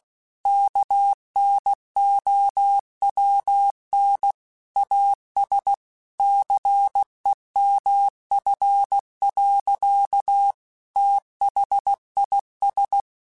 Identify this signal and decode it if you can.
Morse code, heard as a keyed tone, about 16 wpm, 800 Hz. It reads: KNOWN AS CEMF. THIS
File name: Technical Subjects / Technical Subjects-mp3s / XL @15